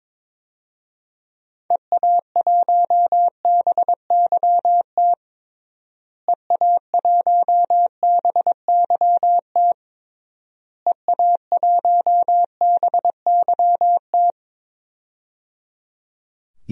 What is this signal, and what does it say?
Morse code, heard as a keyed tone, about 22 wpm, 700 Hz. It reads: EA1BYT EA1BYT EA1BYT
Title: DXCC-100 Call Signs - 6 Characters - Repeated 3x - 22wpm